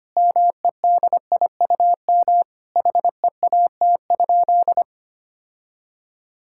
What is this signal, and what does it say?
Morse code, heard as a keyed tone, about 25 wpm, 700 Hz. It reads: MEDIUM HEAT?
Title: Sentences from Top 1000 Words - Review Part 6 - 25wpm